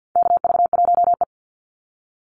Morse code, heard as a keyed tone, about 50 wpm, 700 Hz. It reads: KV1E
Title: Call Signs - 50wpm